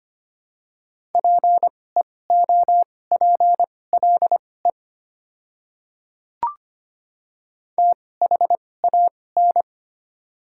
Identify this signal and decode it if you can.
Morse code, heard as a keyed tone, about 25 wpm, spaced as for 18 wpm, 690 Hz. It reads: PEOPLE THAN
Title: 2x Character Spacing - Top 100 Words - 25wpm